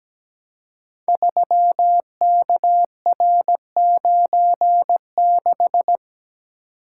Morse code, heard as a keyed tone, about 17 wpm, 700 Hz. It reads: 3KR96
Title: Code Groups - 5 Letter:Numbers - 17wpm